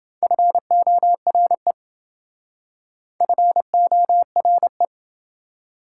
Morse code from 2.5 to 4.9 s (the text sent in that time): FORE